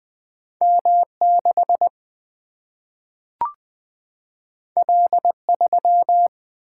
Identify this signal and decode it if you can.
Morse code, heard as a keyed tone, about 20 wpm, 700 Hz. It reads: M6 L3
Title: Single-Letter Single-Number 20wpm